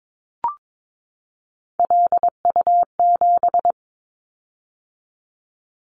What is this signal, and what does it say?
Morse code, heard as a keyed tone, about 22 wpm, 700 Hz. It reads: LU7
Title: Callsign Prefixes for Top 100 DXCC - 22wpm